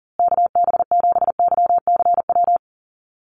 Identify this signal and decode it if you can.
Morse code, heard as a keyed tone, about 40 wpm, 700 Hz. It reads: KB7YCW
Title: USA Call Signs Repeated 2x 40wpm